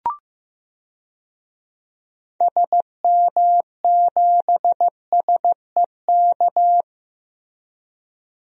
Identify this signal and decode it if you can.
Morse code, heard as a keyed tone, about 15 wpm, 700 Hz. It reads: SM7SEK
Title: DXCC-100 Call Signs - 6 Characters - Repeated 3x - 15wpm